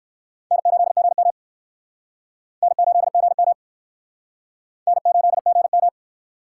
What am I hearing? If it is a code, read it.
Morse code, heard as a keyed tone, about 60 wpm, 700 Hz. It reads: N8GM N8GM N8GM